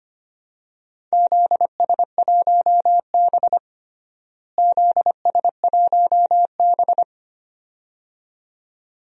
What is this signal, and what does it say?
Morse code, heard as a keyed tone, about 25 wpm, 700 Hz. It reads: ZS1B ZS1B
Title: DXCC-100 Call Signs - 4 Characters - Repeated 2x - 25wpm